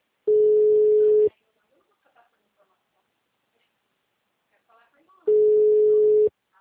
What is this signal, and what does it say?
Ringback tone (Europe, 425 Hz), with speech.